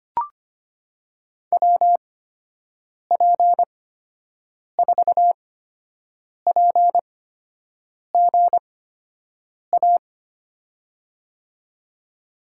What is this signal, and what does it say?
Morse code, heard as a keyed tone, about 25 wpm, spaced as for 7 wpm, 700 Hz. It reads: WP4PGA
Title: USA Call Signs - 8x Character Spacing - 25wpm